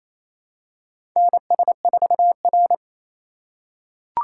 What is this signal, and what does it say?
Morse code, heard as a keyed tone, about 29 wpm, 700 Hz. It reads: NS4R
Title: USA Call Signs - 4 Characters - 2x Character Spacing - 28wpm